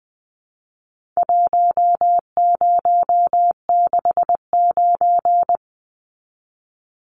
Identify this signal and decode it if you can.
Morse code, heard as a keyed tone, about 20 wpm, 700 Hz. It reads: 1069
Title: Numbers - Contest Style - 4-Digit 20wpm